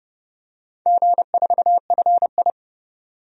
Morse code, heard as a keyed tone, about 32 wpm, 700 Hz. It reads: G4FI
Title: DXCC-100 Call Signs - 4 Characters - 30wpm